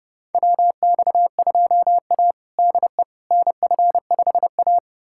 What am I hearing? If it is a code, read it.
Morse code, heard as a keyed tone, about 30 wpm, 700 Hz. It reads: WX2A DE NF5A
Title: USA Callsign DE Callsign - 4 Characters - 30wpm